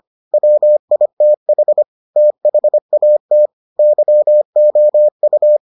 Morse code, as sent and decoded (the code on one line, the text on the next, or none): .-- .. - .... / - .... .- - / -.-- --- ..-
WITH THAT YOU